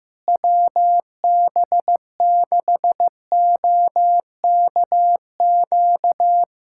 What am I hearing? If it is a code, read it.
Morse code, heard as a keyed tone, about 16 wpm, 700 Hz. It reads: WB6OKQ
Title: USA Callsign DE Callsign 15wpm